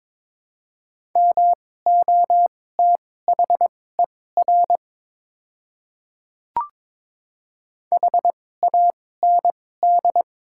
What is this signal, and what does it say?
Morse code, heard as a keyed tone, about 22 wpm, spaced as for 16 wpm, 700 Hz. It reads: MOTHER HAND
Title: 2x Character Spacing - Top 200 Words - 22wpm